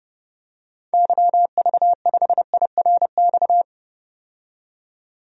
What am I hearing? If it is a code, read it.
Morse code, heard as a keyed tone, about 32 wpm, 700 Hz. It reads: YV5IRX
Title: DXCC-100 Call Signs - 6 Characters - 30wpm